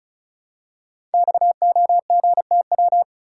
Morse code, heard as a keyed tone, about 36 wpm, 700 Hz. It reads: XOGTW
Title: Code Groups - 5 Letter:Numbers - 35wpm